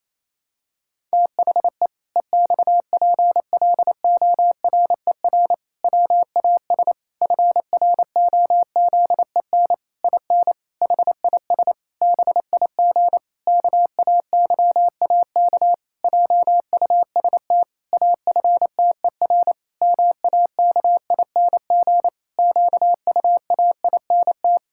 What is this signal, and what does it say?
Morse code, heard as a keyed tone, about 27 wpm, 700 Hz. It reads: THE EXPLORER WAS FROZEN IN HIS BIG KAYAK JUST AFTER MAKING QUAINT
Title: Pangram Challenge v2 - 28wpm